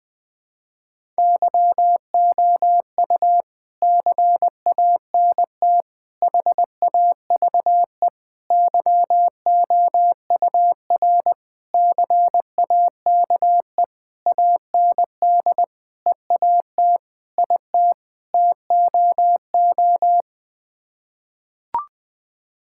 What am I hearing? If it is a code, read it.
Morse code, heard as a keyed tone, about 21 wpm, 700 Hz. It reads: YOU CANT HAVE YOUR CAKE AND EAT IT TOO
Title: Sets of 9 Words - 20wpm